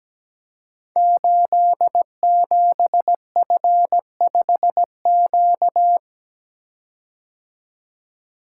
Morse code, heard as a keyed tone, about 17 wpm, 700 Hz. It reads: 87F5Q
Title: Code Groups - 5 Letter:Numbers - 17wpm